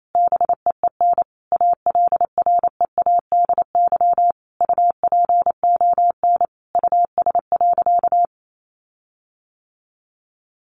Morse code, heard as a keyed tone, about 28 wpm, 700 Hz. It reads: BEEN ALREADY UPON US.
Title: Sentences from Top 1500 Words 28wpm